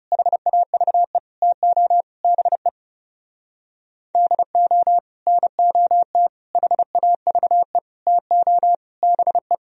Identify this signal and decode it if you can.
Morse code, heard as a keyed tone, about 38 wpm, 700 Hz. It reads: HAVE TO BE DO NOT HAVE TO BE